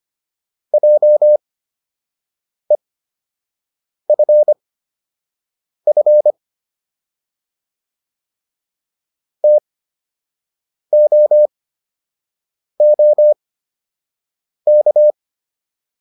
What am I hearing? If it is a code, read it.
Morse code, heard as a keyed tone, about 25 wpm, spaced as for 6 wpm, 600 Hz. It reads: JEFF TOOK